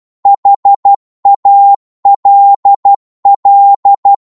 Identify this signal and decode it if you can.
Morse code, heard as a keyed tone, about 12 wpm, 800 Hz. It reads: HALL